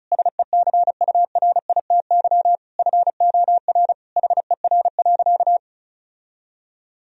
Morse code, heard as a keyed tone, about 36 wpm, 700 Hz. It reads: SECURITY FOR HER.
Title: Sentences from Top 1000 Words - Review Part 4 - 35wpm